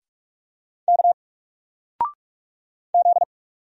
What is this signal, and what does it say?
Morse code, heard as a keyed tone, about 46 wpm, 700 Hz. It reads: K Z
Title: Single Character (Letter:Number:Punctuation:Prosign) 45wpm